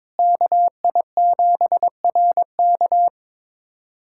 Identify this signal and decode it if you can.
Morse code, heard as a keyed tone, about 22 wpm, 700 Hz. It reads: KI7RK
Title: USA Callsign DE Callsign - 5 Characters - 22wpm